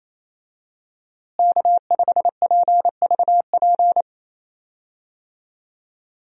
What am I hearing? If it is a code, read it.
Morse code, heard as a keyed tone, about 28 wpm, 700 Hz. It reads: K5PVP
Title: USA Call Signs - 5 Characters - Repeated 3x - 28wpm